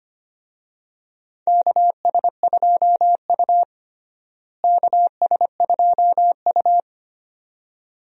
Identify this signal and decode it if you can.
Morse code, heard as a keyed tone, about 25 wpm, 700 Hz. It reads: KS2U KS2U
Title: USA Call Signs - 4 Characters - Repeated 2x - 25wpm